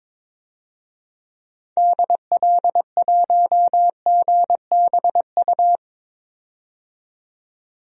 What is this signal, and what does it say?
Morse code, heard as a keyed tone, about 22 wpm, 700 Hz. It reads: DL1GBU
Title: DXCC-100 Call Signs - 6 Characters - 22wpm